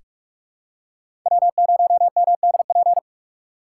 Morse code, heard as a keyed tone, about 46 wpm, 700 Hz. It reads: W0MDP
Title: USA Call Signs - 5 Characters - 4x Character Spacing - 45wpm